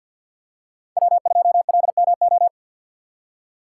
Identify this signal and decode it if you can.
Morse code, heard as a keyed tone, about 47 wpm, 700 Hz. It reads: W2LMO
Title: USA Call Signs - 5 Characters - 6x Character Spacing - 50wpm